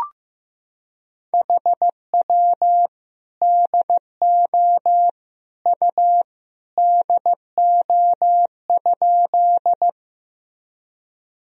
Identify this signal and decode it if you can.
Morse code, heard as a keyed tone, about 15 wpm, 700 Hz. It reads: HW DO U DO?
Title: Sentences from Top 100 Words and Abbreviations 15wpm